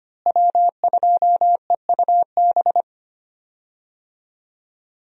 Morse code, heard as a keyed tone, about 25 wpm, 700 Hz. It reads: W2EUB
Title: USA Call Signs - 5 Characters - Repeated 3x - 25wpm